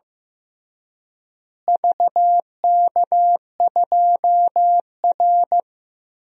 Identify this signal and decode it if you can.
Morse code, heard as a keyed tone, about 15 wpm, 700 Hz. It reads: VK2R
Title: DXCC-100 Call Signs - 4 Characters - Repeated 3x - 15wpm